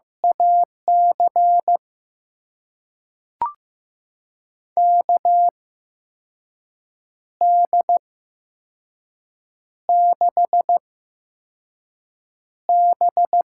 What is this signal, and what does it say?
Morse code, heard as a keyed tone, about 15 wpm, spaced as for 5 wpm, 700 Hz. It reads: AC KD6B
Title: USA Call Signs - 6 Characters - 8x Character Spacing - 15wpm